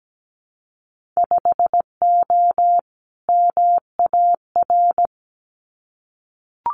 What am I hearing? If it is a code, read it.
Morse code, heard as a keyed tone, about 17 wpm, 700 Hz. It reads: 5O MAR
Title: Winter Field Day Exchanges Encore 17wpm